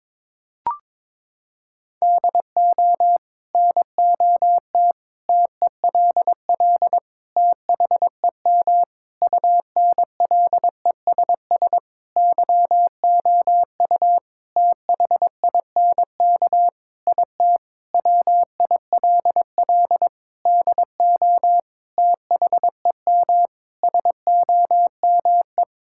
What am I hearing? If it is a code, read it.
Morse code, heard as a keyed tone, about 22 wpm, 700 Hz. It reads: DO NOT TELL THEM UNLESS YOU THINK IT WILL DO THEM SOME
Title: Sentences from Top 1000 Words - Review Part 6 - 22wpm